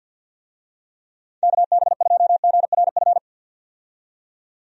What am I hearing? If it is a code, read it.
Morse code, heard as a keyed tone, about 51 wpm, 700 Hz. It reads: KB2GRF
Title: USA Call Signs - 6 Characters - 50wpm